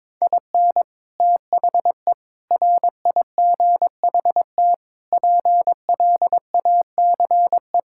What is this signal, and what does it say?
Morse code, heard as a keyed tone, about 22 wpm, 700 Hz. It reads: IN THE RIGHT PLACE
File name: practice / 22wpm / Sets of 8 Words - 22wpm